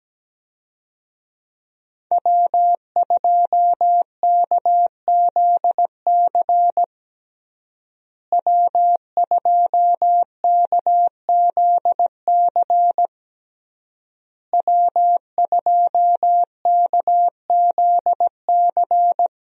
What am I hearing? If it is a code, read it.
Morse code, heard as a keyed tone, about 17 wpm, 700 Hz. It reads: W2KZC W2KZC W2KZC
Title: USA Call Signs - 5 Characters - Repeated 3x - 17wpm